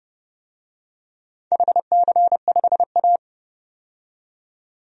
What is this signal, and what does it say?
Morse code, heard as a keyed tone, about 30 wpm, 700 Hz. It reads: HC5A
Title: DXCC-100 Call Signs - 4 Characters - 30wpm